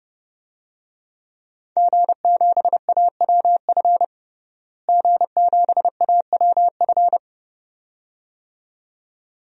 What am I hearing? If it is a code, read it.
Morse code, heard as a keyed tone, about 31 wpm, 700 Hz. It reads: G7AWF G7AWF
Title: DXCC-100 Call Signs - 5 Characters - Repeated 2x - 30wpm